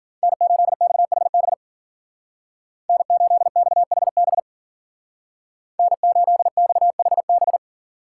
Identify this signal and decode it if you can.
Morse code, heard as a keyed tone, about 48 wpm, 700 Hz. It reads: N8XHB N8XHB N8XHB
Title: Speed Racing - USA 1x3 Call Signs 40wpm